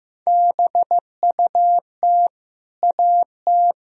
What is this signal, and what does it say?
Morse code, heard as a keyed tone, about 15 wpm, 700 Hz. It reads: BUT AT
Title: Sentences from Top 1000 Words - Review Part 3 - 15wpm